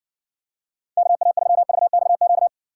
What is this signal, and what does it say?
Morse code, heard as a keyed tone, about 60 wpm, 700 Hz. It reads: KA3VXQ